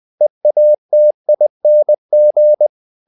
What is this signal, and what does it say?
Morse code, heard as a keyed tone, about 20 wpm, 600 Hz. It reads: EATING
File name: the old man and the sea part 2 20wpm